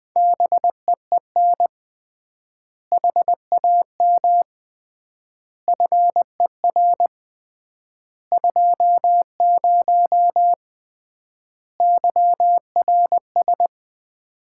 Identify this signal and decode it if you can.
Morse code, heard as a keyed tone, about 20 wpm, 700 Hz. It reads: BEEN HAM FER 20 YRS
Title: Common QSO Phrases - 3x Word-Spacing - 20wpm